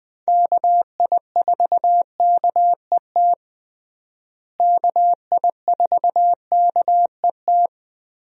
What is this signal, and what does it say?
Morse code, heard as a keyed tone, about 20 wpm, 700 Hz. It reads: KI4KET KI4KET
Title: USA Call Signs - 6 Characters - Repeated 3x - 20wpm